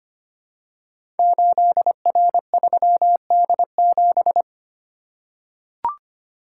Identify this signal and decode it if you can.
Morse code, heard as a keyed tone, about 25 wpm, 700 Hz. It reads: 8R3D7
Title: Code Groups - 5 Letter:Numbers - 25wpm